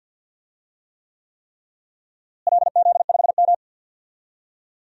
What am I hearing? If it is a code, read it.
Morse code, heard as a keyed tone, about 50 wpm, 700 Hz. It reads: LZ5M
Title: DXCC-100 Call Signs - 4 Characters - Repeated 3x - 50wpm